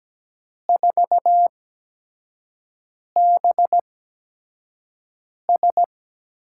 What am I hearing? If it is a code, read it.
Morse code, heard as a keyed tone, about 17 wpm, spaced as for 5 wpm, 700 Hz. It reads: 4BS
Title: USA Call Signs - 8x Character Spacing - 17wpm